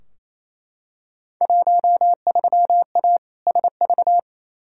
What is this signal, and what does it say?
Morse code, heard as a keyed tone, about 28 wpm, 700 Hz. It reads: 13A SV
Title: Field Day Exchanges Encore - 28wpm